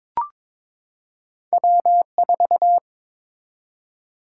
Morse code, heard as a keyed tone, about 22 wpm, 700 Hz. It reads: W4
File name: practice / 22wpm / Two Letter-Number Combinations - 22wpm